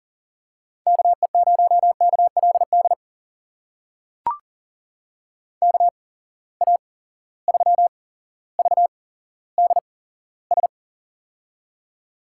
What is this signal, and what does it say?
Morse code, heard as a keyed tone, about 40 wpm, spaced as for 17 wpm, 700 Hz. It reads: KE0KLD KA3VDS
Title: USA Call Signs - 6 Characters - 8x Character Spacing - 40wpm